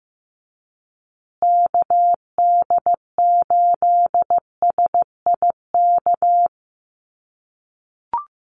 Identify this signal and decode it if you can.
Morse code, heard as a keyed tone, about 15 wpm, 700 Hz. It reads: KD8SIK
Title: USA Call Signs Repeated 3x 15wpm